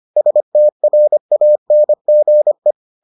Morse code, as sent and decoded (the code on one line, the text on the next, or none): ... - .-. .- -. --. .
STRANGE